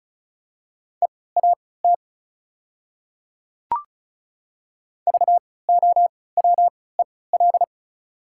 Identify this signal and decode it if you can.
Morse code, heard as a keyed tone, about 36 wpm, spaced as for 20 wpm, 690 Hz. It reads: EAT VOWEL
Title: 3x Character Spacing - Top 500 Words - 35wpm